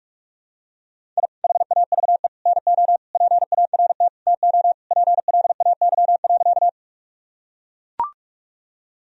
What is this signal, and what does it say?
Morse code, heard as a keyed tone, about 42 wpm, 700 Hz. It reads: I HAVE NO PART TO PLAY.
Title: Sentences from Top 1000 Words - Review Part 1 - 45wpm